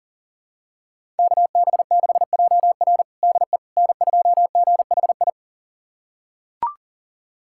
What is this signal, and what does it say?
Morse code, heard as a keyed tone, about 37 wpm, 700 Hz. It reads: KB6JR DE N2GHI
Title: USA Callsign DE Callsign 40wpm